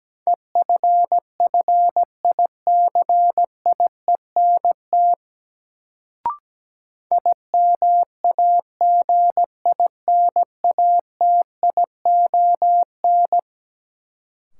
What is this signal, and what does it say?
Morse code, heard as a keyed tone, about 17 wpm, 700 Hz. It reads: EFFICIENT IMAGINATION
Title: Sentences from Top 1300 Words - New Word Drill - 17wpm